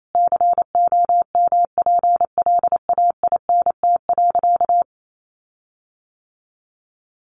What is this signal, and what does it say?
Morse code, heard as a keyed tone, about 28 wpm, 700 Hz. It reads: COMPLAINT.
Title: Sentences from Top 1400 Words 28wpm